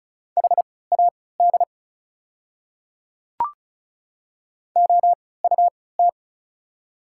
Morse code, heard as a keyed tone, about 36 wpm, spaced as for 20 wpm, 690 Hz. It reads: HAD OUT
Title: 3x Character Spacing - Top 100 Words - 35wpm